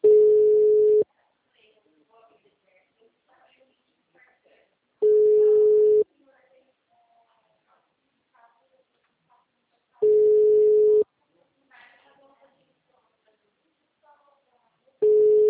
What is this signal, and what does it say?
Ringback tone (Europe, 425 Hz), with speech.
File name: out- -2007- - - .67224